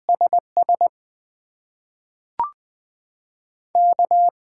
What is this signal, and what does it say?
Morse code, heard as a keyed tone, about 20 wpm, spaced as for 5 wpm, 700 Hz. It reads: SSK
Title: USA Call Signs - 6x Character Spacing - 20wpm